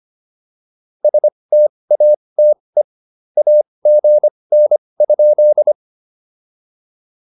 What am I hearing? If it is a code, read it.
Morse code, heard as a keyed tone, about 25 wpm, spaced as for 20 wpm, 600 Hz. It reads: STATE AGN?
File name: POTA206 20